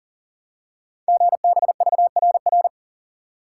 Morse code, heard as a keyed tone, about 43 wpm, 700 Hz. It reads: GBVRR
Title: Code Groups - 5 Letters - 40wpm